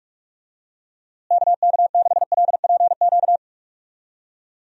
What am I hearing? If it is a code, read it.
Morse code, heard as a keyed tone, about 44 wpm, 700 Hz. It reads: KK6LPQ
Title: USA Call Signs - 6 Characters - Repeated 2x - 45wpm